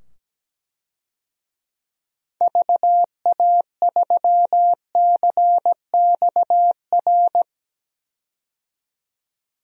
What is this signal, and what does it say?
Morse code, heard as a keyed tone, about 17 wpm, 700 Hz. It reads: VA3CXR
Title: DXCC-100 Call Signs - 6 Characters - 17wpm